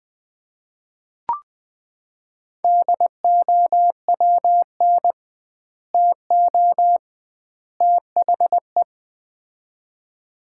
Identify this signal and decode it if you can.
Morse code, heard as a keyed tone, about 21 wpm, 700 Hz. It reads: DOWN TO THE
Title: 2x Word-Spacing - Sets of 3 Words - 20wpm